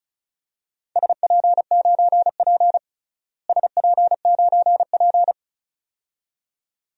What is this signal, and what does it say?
Morse code, heard as a keyed tone, about 34 wpm, 700 Hz. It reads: SP9P SP9P
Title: DXCC-100 Call Signs - 4 Characters - Repeated 2x - 35wpm